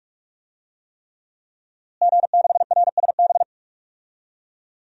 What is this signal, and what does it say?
Morse code, heard as a keyed tone, about 46 wpm, 700 Hz. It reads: G6RSB
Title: DXCC-100 Call Signs - 5 Characters - Repeated 3x - 45wpm